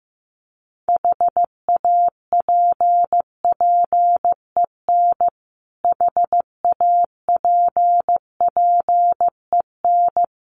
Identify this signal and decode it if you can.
Morse code, heard as a keyed tone, about 15 wpm, 700 Hz. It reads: HAPPEN HAPPEN
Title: Top 500 Words Repeated 15wpm